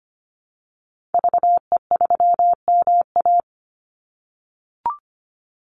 Morse code, heard as a keyed tone, about 25 wpm, 700 Hz. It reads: VE3MA
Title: Call Signs - 25wpm